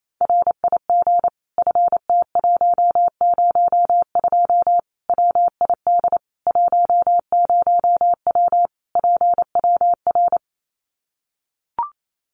Morse code, heard as a keyed tone, about 28 wpm, 700 Hz. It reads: RIG FT102 WID 10W PWR
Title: Common QSO Phrases 28wpm